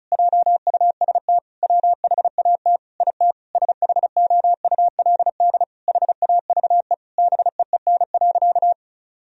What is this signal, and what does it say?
Morse code, heard as a keyed tone, about 36 wpm, 700 Hz. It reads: JUST WHAT IT SHOULD HAVE BEEN.
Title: Sentences from Top 1000 Words - Review Part 4 - 35wpm